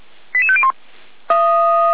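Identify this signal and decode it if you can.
Five-tone selective call (ZVEI), heard as quick runs of five tones, tones of about 70 ms. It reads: 88501
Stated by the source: ZVEI five-tone selective call 88150 received by SDR.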